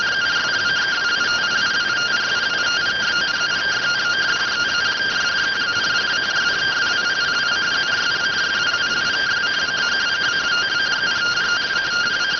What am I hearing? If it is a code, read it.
Two-tone teleprinter signal, about 170 Hz shift.